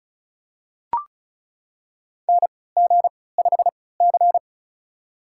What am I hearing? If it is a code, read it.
Morse code, heard as a keyed tone, about 34 wpm, spaced as for 20 wpm, 700 Hz. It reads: NG5C
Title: USA Call Signs - 4 Characters - 3x Character Spacing - 35wpm